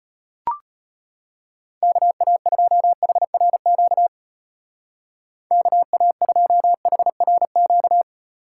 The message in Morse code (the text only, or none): KA2HRQ KA2HRQ